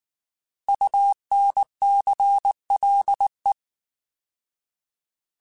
Morse code, heard as a keyed tone, about 19 wpm, 800 Hz. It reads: UNCLE